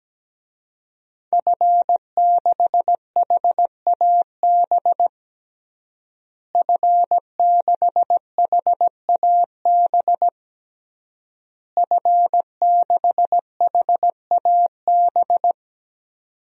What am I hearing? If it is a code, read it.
Morse code, heard as a keyed tone, about 17 wpm, 700 Hz. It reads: F6HAB F6HAB F6HAB
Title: DXCC-100 Call Signs - 5 Characters - Repeated 3x - 17wpm